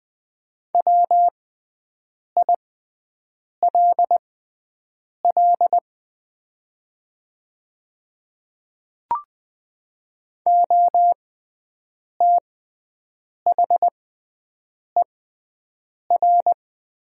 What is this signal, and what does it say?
Morse code, heard as a keyed tone, about 20 wpm, spaced as for 7 wpm, 700 Hz. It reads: WILL OTHER
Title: 6x Character Spacing - Top 100 Words - 20wpm